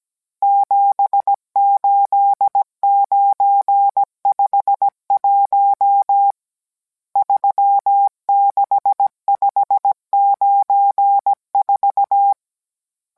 Morse code, heard as a keyed tone, about 17 wpm, 800 Hz. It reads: 78951 36594